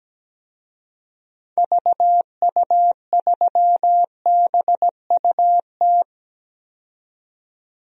Morse code, heard as a keyed tone, about 17 wpm, 700 Hz. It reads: VU3BUT
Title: DXCC-100 Call Signs - 6 Characters - 17wpm